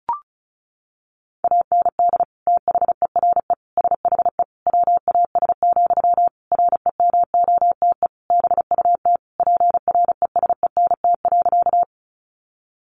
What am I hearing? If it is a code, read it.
Morse code, heard as a keyed tone, about 34 wpm, 700 Hz. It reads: AND THERE SHE WAS, REMOTE BUT PRESENT.
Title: Sentences from Top 1300 Words 35wpm